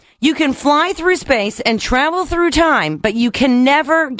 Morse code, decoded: none